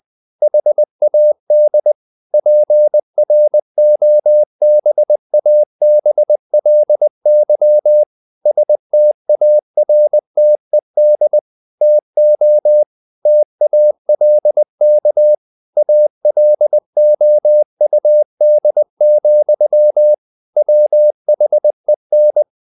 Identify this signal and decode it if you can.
Morse code, heard as a keyed tone, about 20 wpm, 600 Hz. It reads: HAD PROBABLY STARTED TO TALK ALOUD, WHEN
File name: the old man and the sea / the old man and the sea part 1 20wpm